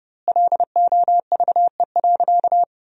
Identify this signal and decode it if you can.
Morse code, heard as a keyed tone, about 30 wpm, 700 Hz. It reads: LOVE.
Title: 2x Word-Spacing - Sentences from Top 400 Words - 30wpm